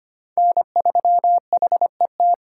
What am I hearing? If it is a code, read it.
Morse code, heard as a keyed tone, about 25 wpm, 700 Hz. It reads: N3HET
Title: USA Call Signs - 5 Characters - 4x Character Spacing - 25wpm